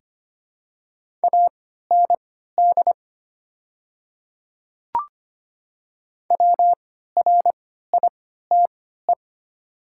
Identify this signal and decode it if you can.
Morse code, heard as a keyed tone, about 25 wpm, spaced as for 14 wpm, 690 Hz. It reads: AND WRITE
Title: 3x Character Spacing - Top 100 Words - 25wpm